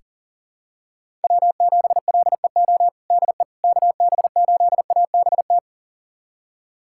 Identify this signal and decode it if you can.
Morse code, heard as a keyed tone, about 40 wpm, 700 Hz. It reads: W7LEO DE KB8ABT
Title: USA Callsign DE Callsign 40wpm